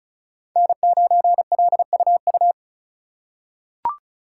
Morse code, heard as a keyed tone, about 36 wpm, 690 Hz. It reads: N9LUU
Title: USA Call Signs - 5 Characters - 3x Character Spacing - 35wpm